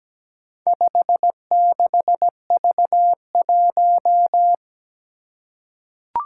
Morse code, heard as a keyed tone, about 17 wpm, 700 Hz. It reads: 56V1
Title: Four Challenging Letter-Number Combinations 17wpm